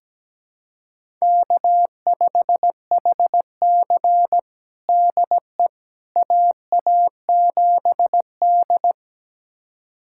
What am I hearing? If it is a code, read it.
Morse code, heard as a keyed tone, about 17 wpm, 700 Hz. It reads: K5HC DE AA7D
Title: USA Callsign DE Callsign - 4 Characters - 17wpm